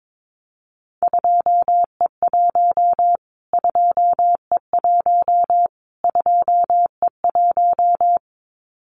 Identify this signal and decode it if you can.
Morse code, heard as a keyed tone, about 22 wpm, 700 Hz. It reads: 2E1 2E1 2E1
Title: Callsign Prefixes for Top 25 DXCC - Repeated 3x - 22wpm